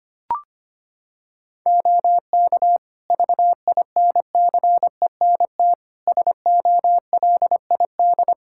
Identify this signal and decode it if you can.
Morse code, heard as a keyed tone, about 25 wpm, 700 Hz. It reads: OK VINCENT SOLID
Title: Practice QSO - Exchange 3 - More Info - 25wpm